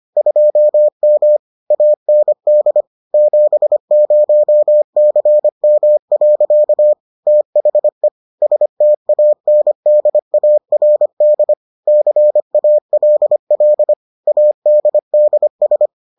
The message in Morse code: 2M AND 70CM. THE STANDARD CALL ADDS

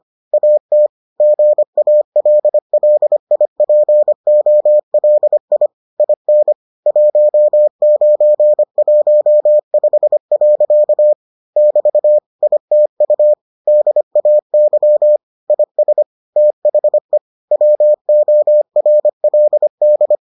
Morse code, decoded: AT GALLIPOLI IN 1915. <BT> ITU DAY IS THE WORLD

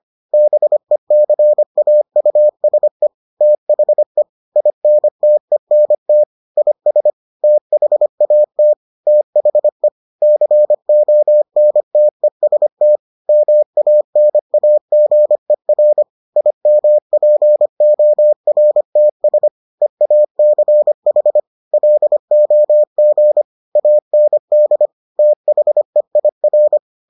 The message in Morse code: BECAUSE THE INTENT IS THAT THE CONTEST MANAGER IMPORTS EACH LOG AND THEIR

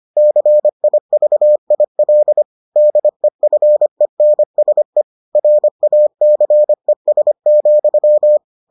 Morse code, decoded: CIVIL DEFENSE RACES,

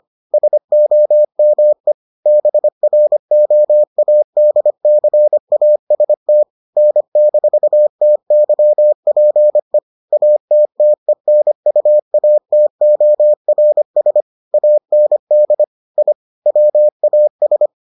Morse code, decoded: SOME BROADCAST N-TYPE ATTENUATORS AND I WAS

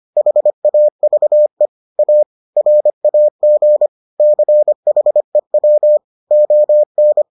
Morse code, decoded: HAVE A RAG CHEW ON

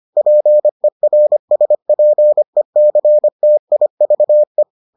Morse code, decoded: PERSPECTIVE

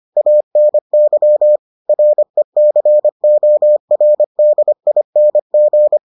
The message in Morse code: ANY RECORDING

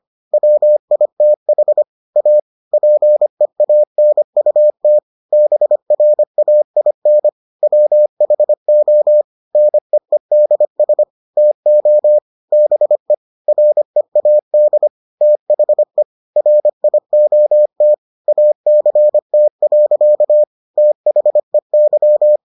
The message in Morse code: WITH A PEANUT BRAIN WHO NEEDS TO BE READ THE RIOT ACT. THEY

